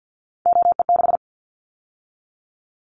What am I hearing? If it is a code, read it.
Morse code, heard as a keyed tone, about 50 wpm, 700 Hz. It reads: OE6
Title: Callsign Prefixes for Top 50 DXCC - 50wpm